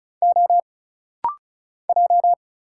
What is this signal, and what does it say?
Morse code, heard as a keyed tone, about 36 wpm, 700 Hz. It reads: O J
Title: Single Letter-Number - Rapid-Fire 35wpm